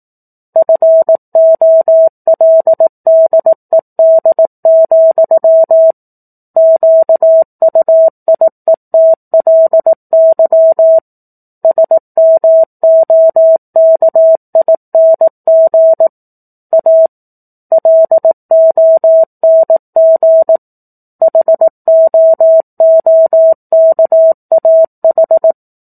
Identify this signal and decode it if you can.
Morse code, heard as a keyed tone, about 18 wpm, 650 Hz. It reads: FOLDED, QUIETLY SMOKING A LONG HOOKAH
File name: 04-Chapter-04 18 WPM